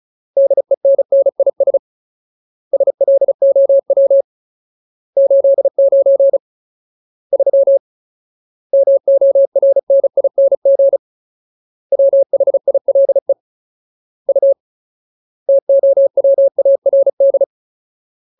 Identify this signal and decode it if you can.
Morse code, heard as a keyed tone, about 34 wpm, 550 Hz. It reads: DENNIS SLOW 89 3 MORNING WHILE U TOWARD
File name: Ultimate Word List / 35WPM Ultimate